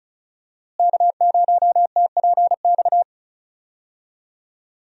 Morse code, heard as a keyed tone, about 36 wpm, 700 Hz. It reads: K0TPX
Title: USA Call Signs - 5 Characters - Repeated 2x - 35wpm